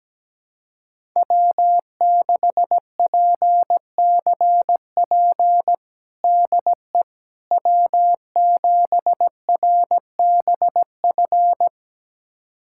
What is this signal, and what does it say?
Morse code, heard as a keyed tone, about 17 wpm, 700 Hz. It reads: W6PCP DE W7RBF
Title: USA Callsign DE Callsign - 5 Characters - 17wpm